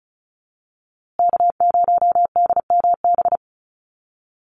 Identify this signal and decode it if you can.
Morse code, heard as a keyed tone, about 34 wpm, 700 Hz. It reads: K0DMB
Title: USA Call Signs Repeated 2x 35wpm